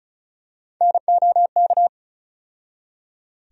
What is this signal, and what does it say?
Morse code, heard as a keyed tone, about 36 wpm, 700 Hz. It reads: NOK
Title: USA Call Sign Postfixes - 3 Letters 35wpm